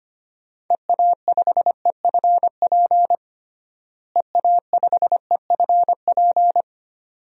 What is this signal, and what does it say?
Morse code, heard as a keyed tone, about 25 wpm, 700 Hz. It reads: EA5EFP EA5EFP
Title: DXCC-100 Call Signs - 6 Characters - Repeated 2x - 25wpm